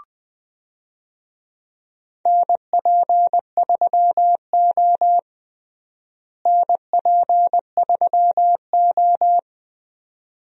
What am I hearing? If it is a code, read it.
Morse code, heard as a keyed tone, about 20 wpm, 700 Hz. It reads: NP3O NP3O
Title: USA Call Signs - 4 Characters - Repeated 2x - 20wpm